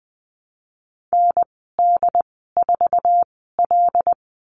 Morse code, heard as a keyed tone, about 20 wpm, spaced as for 15 wpm, 700 Hz. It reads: ND4L
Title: 076 Lesson 13g - Letter L - Call Signs 20wpm